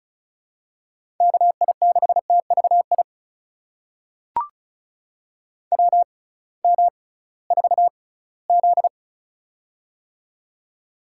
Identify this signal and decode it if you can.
Morse code, heard as a keyed tone, about 36 wpm, spaced as for 20 wpm, 700 Hz. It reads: KI6TVI WM4Z
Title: USA Call Signs - 6x Character Spacing - 35wpm